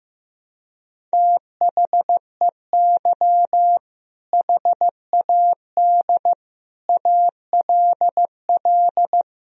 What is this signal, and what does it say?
Morse code, heard as a keyed tone, about 15 wpm, 700 Hz. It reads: THEY HAD ALL
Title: Sentences from Top 200 Words and Abbreviations 15wpm